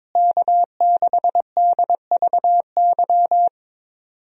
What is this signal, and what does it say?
Morse code, heard as a keyed tone, about 22 wpm, 700 Hz. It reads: K6DVY
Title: USA Call Signs - 5 Characters - Repeated 3x - 22wpm